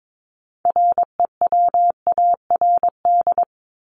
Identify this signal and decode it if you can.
Morse code, heard as a keyed tone, about 22 wpm, 700 Hz. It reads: REWARD
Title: Top 1751-2000 Words 22wpm